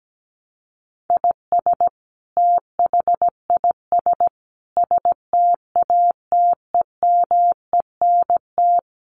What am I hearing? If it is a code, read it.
Morse code, heard as a keyed tone, about 17 wpm, 700 Hz. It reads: IS THIS STATEMENT
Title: Sentences from Top 1200 Words 17wpm